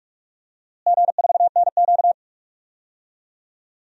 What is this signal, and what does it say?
Morse code, heard as a keyed tone, about 46 wpm, 700 Hz. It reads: G4NQ
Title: DXCC-100 Call Signs - 4 Characters - Repeated 2x - 45wpm